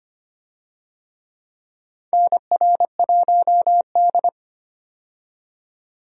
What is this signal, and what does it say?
Morse code, heard as a keyed tone, about 25 wpm, 700 Hz. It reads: NR1D
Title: USA Call Signs - 4 Characters - 25wpm